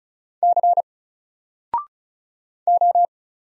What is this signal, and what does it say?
Morse code, heard as a keyed tone, about 34 wpm, 700 Hz. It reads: C O
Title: Single Letter-Number Alt 35wpm